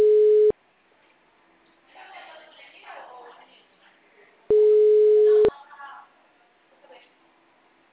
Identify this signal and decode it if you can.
Ringback tone (Europe, 425 Hz), with speech.